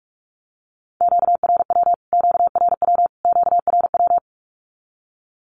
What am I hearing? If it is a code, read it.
Morse code, heard as a keyed tone, about 46 wpm, 700 Hz. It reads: QRW QRW QRW
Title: Q-Codes Repeated 3x 45wpm